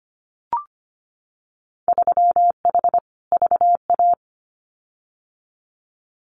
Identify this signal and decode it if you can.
Morse code, heard as a keyed tone, about 25 wpm, 700 Hz. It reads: 3H VA
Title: Winter Field Day Exchanges 25wpm